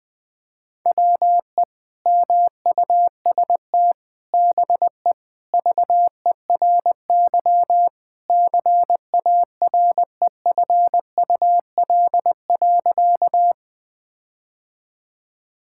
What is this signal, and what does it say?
Morse code, heard as a keyed tone, about 20 wpm, 700 Hz. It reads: WE MUST BE VERY CAREFUL.
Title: Sentences from Top 1600 Words 20wpm